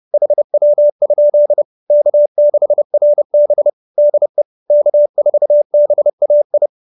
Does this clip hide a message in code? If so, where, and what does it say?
Yes: Morse code from 0.1 to 6.7 s, heard as a keyed tone, about 30 wpm, 590 Hz. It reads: HW? K6RB DE K4BAI